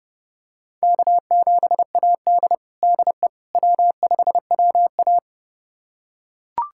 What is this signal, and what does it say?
Morse code, heard as a keyed tone, about 30 wpm, 700 Hz. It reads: K7AD DE W5WA
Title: USA Callsign DE Callsign - 4 Characters - 30wpm